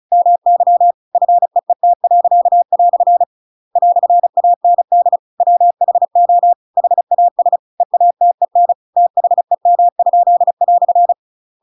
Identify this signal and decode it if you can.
Morse code, heard as a keyed tone, about 35 wpm, 700 Hz. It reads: MY FEET." "AND WHO HAS EATEN THEM?"
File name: Pinocchio / Pinocchio ch7 35 35 ik1hge